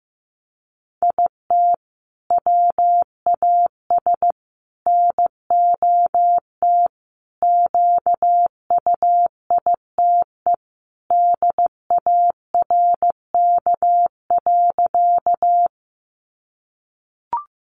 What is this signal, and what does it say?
Morse code, heard as a keyed tone, about 14 wpm, 700 Hz. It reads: IT WAS NOT QUITE DARK.
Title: Sentences from Top 700 Words 15wpm